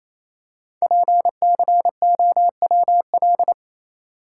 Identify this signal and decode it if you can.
Morse code, heard as a keyed tone, about 28 wpm, 700 Hz. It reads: PCOWL
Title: Code Groups - 5 Letters - 28wpm